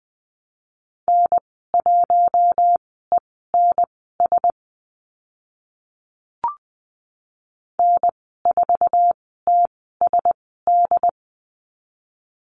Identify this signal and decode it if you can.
Morse code, heard as a keyed tone, about 20 wpm, spaced as for 14 wpm, 700 Hz. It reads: N1ENS N4TSD
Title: 069 Lesson 12g - Letter d - Call Signs 20wpm